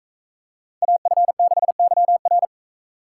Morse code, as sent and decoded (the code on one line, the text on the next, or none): .- ..-. -.... -.-- .-.
AF6YR